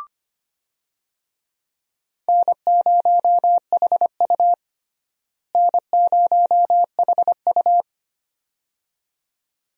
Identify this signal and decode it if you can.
Morse code, heard as a keyed tone, about 25 wpm, 700 Hz. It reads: N0HU N0HU
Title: USA Call Signs - 4 Characters - Repeated 2x - 25wpm